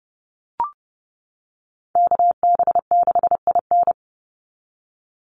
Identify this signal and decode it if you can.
Morse code, heard as a keyed tone, about 30 wpm, 700 Hz. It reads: KB6IN
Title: USA Call Signs 30wpm